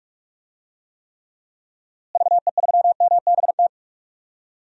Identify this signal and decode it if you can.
Morse code, heard as a keyed tone, about 46 wpm, 700 Hz. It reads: VE3MBT
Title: DXCC-100 Call Signs - 6 Characters - Repeated 3x - 45wpm